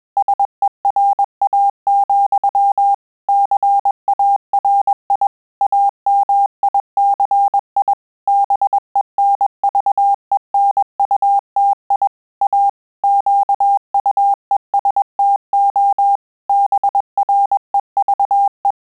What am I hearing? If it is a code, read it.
Morse code, heard as a keyed tone, about 21 wpm, 800 Hz. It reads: SERA, CARI AMICI BENVENUTI A QUESTO BREVE
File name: Telegraphy-group / breve testo midi